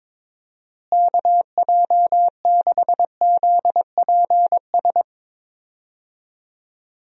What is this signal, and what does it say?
Morse code, heard as a keyed tone, about 22 wpm, 700 Hz. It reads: KJ6ZPS
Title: USA Call Signs - 6 Characters - Repeated 2x - 22wpm